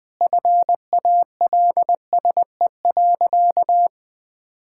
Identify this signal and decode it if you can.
Morse code, heard as a keyed tone, about 20 wpm, 700 Hz. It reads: FALSE.
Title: Sentences from Top 1600 Words 20wpm